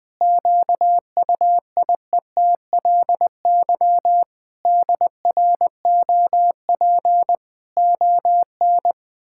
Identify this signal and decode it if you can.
Morse code, heard as a keyed tone, about 20 wpm, 700 Hz. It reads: QUIETLY DROP ON
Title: Pangram Challenge v2 - 20wpm